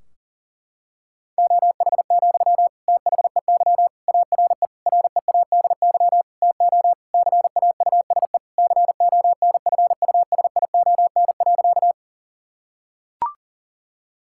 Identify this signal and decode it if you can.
Morse code, heard as a keyed tone, about 40 wpm, 700 Hz. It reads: OH, THEY ARE READY TO CAUSE CONFUSION.
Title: Sentences from Top 1600 Words 40wpm